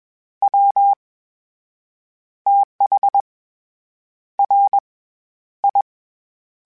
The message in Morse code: WTH RI

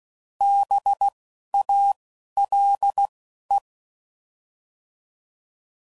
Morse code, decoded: B A L E

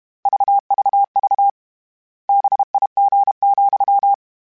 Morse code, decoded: VVV BIG,

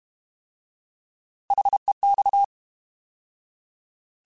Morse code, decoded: HEX

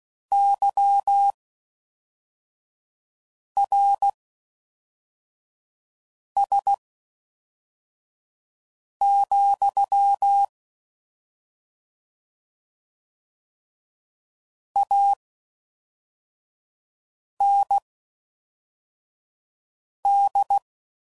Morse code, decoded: YRS, AND

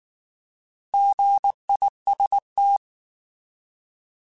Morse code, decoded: GIST

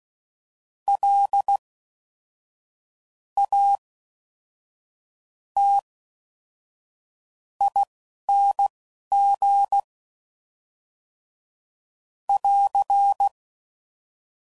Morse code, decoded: LATING <AR>

